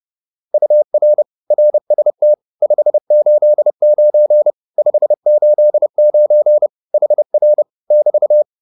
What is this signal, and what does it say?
Morse code, heard as a keyed tone, about 30 wpm, 600 Hz. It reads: UR RST 589 589 HR <BT>
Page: advanced-practice-files